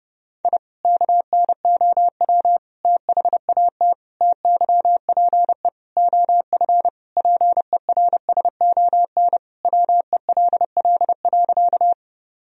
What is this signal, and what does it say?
Morse code, heard as a keyed tone, about 30 wpm, 700 Hz. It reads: I KNOW THAT TYPE OF PERSON WELL.
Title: Sentences from Top 1000 Words - Review Part 3 - 30wpm